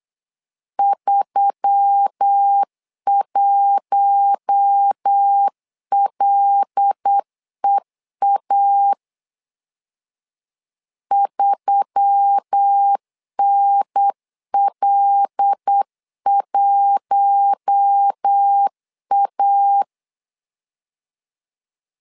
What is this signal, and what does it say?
Morse code, heard as a keyed tone, about 8 wpm, 800 Hz. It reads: 31LEA 3NL1A
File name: LEZIONI / 5 LBVC / 5 L 8wpm 2831